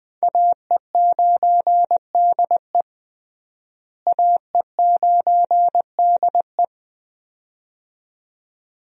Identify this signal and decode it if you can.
Morse code, heard as a keyed tone, about 21 wpm, 700 Hz. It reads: AE9DE AE9DE
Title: USA Call Signs - 5 Characters - Repeated 2x - 20wpm